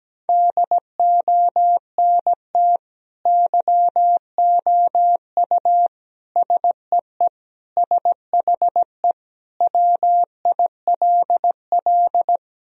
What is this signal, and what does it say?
Morse code, heard as a keyed tone, about 17 wpm, 700 Hz. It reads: DONT YOU SEE SHE WILL
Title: Sentences from Top 1000 Words - Review Part 2 - 17wpm